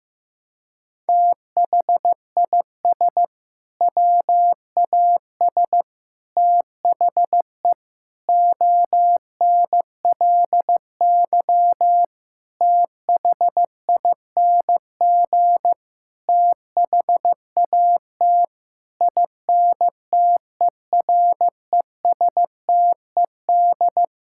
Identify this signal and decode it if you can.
Morse code, heard as a keyed tone, about 15 wpm, 700 Hz. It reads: THIS WAS THE ONLY THING THAT INTERESTED
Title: Sentences from Top 1000 Words - Review Part 6 - 15wpm